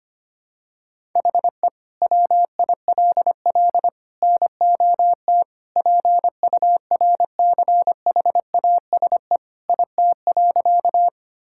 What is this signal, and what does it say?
Morse code, heard as a keyed tone, about 25 wpm, 700 Hz. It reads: HE WILL NOT PURCHASE IT.
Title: Sentences from Top 1000 Words - Review Part 5 - 25wpm